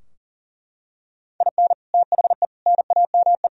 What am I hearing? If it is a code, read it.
Morse code, heard as a keyed tone, about 40 wpm, 700 Hz. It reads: IN THE NAME
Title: Sentences from Top 1000 Words - Review Part 3 - 40wpm